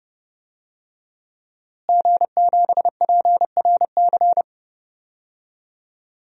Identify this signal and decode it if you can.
Morse code, heard as a keyed tone, about 30 wpm, 700 Hz. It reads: G7PRC
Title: DXCC-100 Call Signs - 5 Characters - Repeated 2x - 30wpm